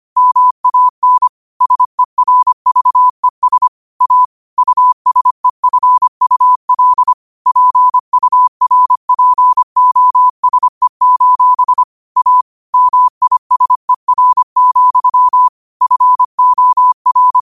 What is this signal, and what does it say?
Morse code, heard as a keyed tone, about 25 wpm, 1000 Hz. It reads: MAN SERVES A USEFUL PURPOSE: A MISER, FOR
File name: qotdfiles / 25wpm